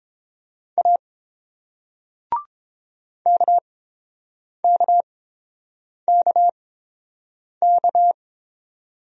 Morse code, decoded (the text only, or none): A KKKK